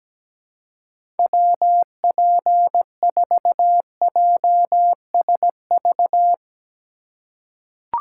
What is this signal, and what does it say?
Morse code, heard as a keyed tone, about 17 wpm, 700 Hz. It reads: WP4JSV
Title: USA Call Signs - 6 Characters - 2x Character Spacing - 17wpm